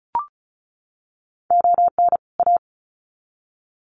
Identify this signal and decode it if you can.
Morse code, heard as a keyed tone, about 34 wpm, 700 Hz. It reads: ON A
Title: Sets of 2 Words 35wpm